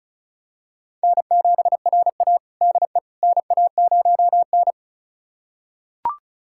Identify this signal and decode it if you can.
Morse code, heard as a keyed tone, about 34 wpm, 700 Hz. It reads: N7RA DE NA0N
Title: USA Callsign DE Callsign - 4 Characters - 35wpm